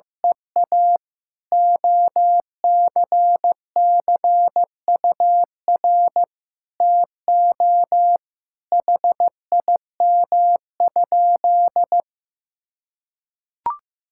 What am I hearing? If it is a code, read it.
Morse code, heard as a keyed tone, about 16 wpm, 700 Hz. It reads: EA OCCUR TO HIM?
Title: Sentences from Top 1000 Words - Review Part 6 - 15wpm